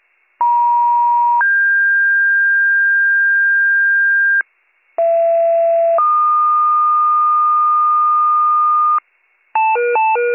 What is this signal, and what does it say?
Two-tone page, heard as a one-second tone, then a three-second tone, about 950 then 1600 Hz, about 670 then 1220 Hz, with noise. It is heard from 0.4 to 9.0 s.